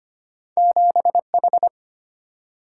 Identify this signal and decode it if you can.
Morse code, heard as a keyed tone, about 25 wpm, 700 Hz. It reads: 7H
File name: Two Letter-Number Combinations - 25wpm